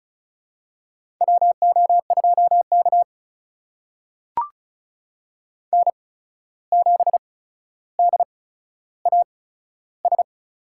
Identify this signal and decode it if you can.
Morse code, heard as a keyed tone, about 36 wpm, spaced as for 14 wpm, 700 Hz. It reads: WO2K N7DAS
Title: USA Call Signs - 8x Character Spacing - 35wpm